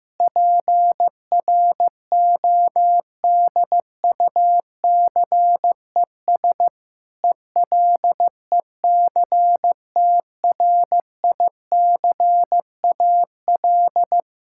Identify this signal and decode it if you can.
Morse code, heard as a keyed tone, about 16 wpm, 700 Hz. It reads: PRODUCES ELECTRICAL
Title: Sentences from Top 1000 Words - Review Part 6 - 15wpm